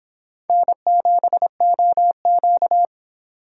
Morse code, heard as a keyed tone, about 26 wpm, 700 Hz. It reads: N7OQ